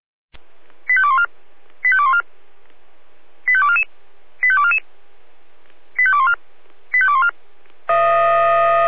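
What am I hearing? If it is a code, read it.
Five-tone selective call (ZVEI), heard as quick runs of five tones, tones of about 70 ms. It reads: 85214 85214 85244 85244 85214 85214
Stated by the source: ZVEI five-tone call 88133 received by SDR.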